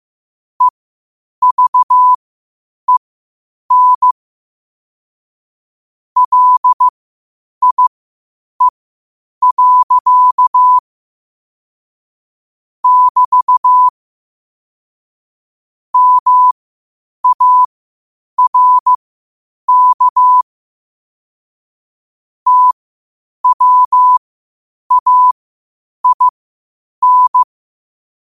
Morse code, heard as a keyed tone, about 15 wpm, spaced as for 8 wpm, 1000 Hz. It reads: EVEN LIE. <BT> MARK TWAIN
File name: qotdfiles / 8wpm